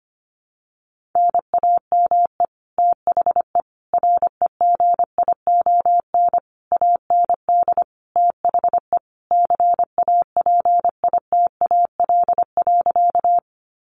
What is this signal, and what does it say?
Morse code, heard as a keyed tone, about 25 wpm, 700 Hz. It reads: NAME THE REGION AND THE CAPITAL.
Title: Sentences from Top 1500 Words 25wpm